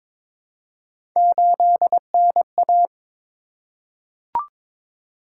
Morse code, heard as a keyed tone, about 23 wpm, 700 Hz. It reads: 8NA
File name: practice / 22wpm / Three Letter-Number Combinations - 22wpm